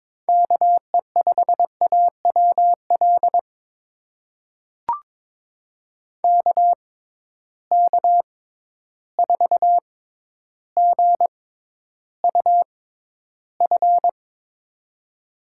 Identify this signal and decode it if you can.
Morse code, heard as a keyed tone, about 22 wpm, spaced as for 11 wpm, 700 Hz. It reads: KE5AWL KK4GUF
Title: USA Call Signs - 6 Characters - 6x Character Spacing - 22wpm